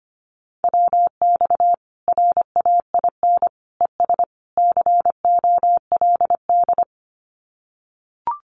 Morse code, heard as a keyed tone, about 25 wpm, 700 Hz. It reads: WX RAIN ES COLD
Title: Common QSO Phrases 25wpm